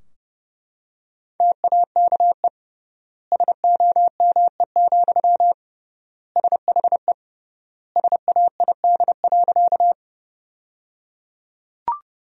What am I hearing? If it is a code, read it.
Morse code, heard as a keyed tone, about 29 wpm, 700 Hz. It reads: TAKE SOME, SHE SAID.
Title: 3x Word-Spacing - Sentences from Top 200 Words - 30wpm